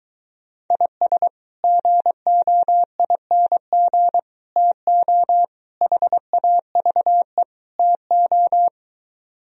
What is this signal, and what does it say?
Morse code, heard as a keyed tone, about 23 wpm, 700 Hz. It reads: IS GOING TO HAVE TO